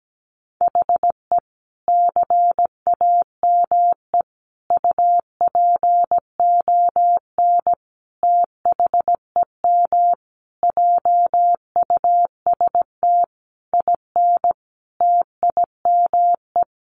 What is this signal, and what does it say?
Morse code, heard as a keyed tone, about 17 wpm, 700 Hz. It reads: HE CAME UPON THEM JUST IN TIME
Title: Sentences from Top 1100 Words 17wpm